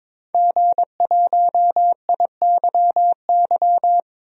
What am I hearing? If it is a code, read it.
Morse code, heard as a keyed tone, about 21 wpm, 700 Hz. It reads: G1IYY
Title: DXCC-100 Call Signs - 5 Characters - 22wpm